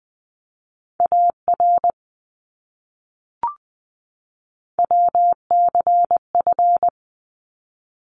Morse code, heard as a keyed tone, about 20 wpm, 700 Hz. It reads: AR WCF
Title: ARRL Sections 20wpm